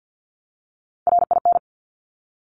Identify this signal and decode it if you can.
Morse code, heard as a keyed tone, about 50 wpm, 700 Hz. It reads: RIN
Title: Top Three Letter Combinations in Words 50wpm